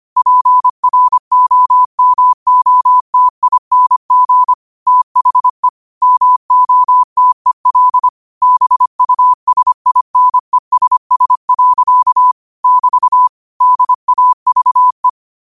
Morse code, heard as a keyed tone, about 25 wpm, 1000 Hz. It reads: PROMOTING THE MOTEL BUSINESS. <BT> DAVE
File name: 25wpm